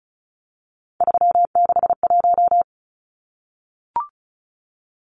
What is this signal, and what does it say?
Morse code, heard as a keyed tone, about 34 wpm, 700 Hz. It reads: 361
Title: Numbers - 3-Digit 35wpm